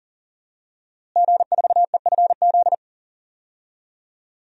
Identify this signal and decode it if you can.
Morse code, heard as a keyed tone, about 37 wpm, 700 Hz. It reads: G4EFZ
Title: DXCC-100 Call Signs - 5 Characters - Repeated 2x - 40wpm